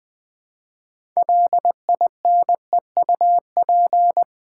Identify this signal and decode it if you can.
Morse code, heard as a keyed tone, about 20 wpm, 700 Hz. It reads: LINEUP
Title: 6 Letter Words - First 3 Letters Given - 20wpm